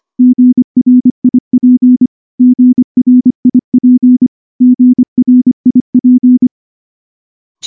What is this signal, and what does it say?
Morse code, heard as a keyed tone, about 23 wpm, 250 Hz. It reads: GRIP GRIP GRIP
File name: morse / morse-training-25WPM